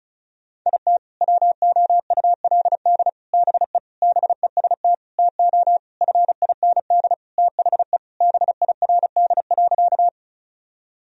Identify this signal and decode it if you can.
Morse code, heard as a keyed tone, about 36 wpm, 700 Hz. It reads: IT WOULD BE BEST TO FIND THE BIRD.
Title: Sentences from Top 1000 Words - Review Part 2 - 35wpm